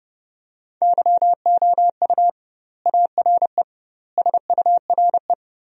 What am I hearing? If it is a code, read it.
Morse code, heard as a keyed tone, about 32 wpm, 700 Hz. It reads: YOU ARE SURE
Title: 2x Word-Spacing - Sentences from Top 300 Words - 30wpm